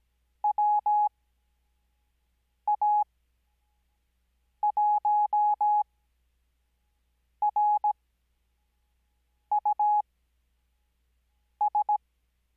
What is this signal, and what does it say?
Morse code, heard as a keyed tone, about 17 wpm, spaced as for 5 wpm, 840 Hz. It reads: WA1RUS
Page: Farnsworth Spacing: 15WPM Character speed Spaced at 5WPM.